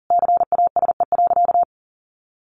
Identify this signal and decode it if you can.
Morse code, heard as a keyed tone, about 40 wpm, 700 Hz. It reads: CASE.
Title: Sentences from Top 1400 Words 40wpm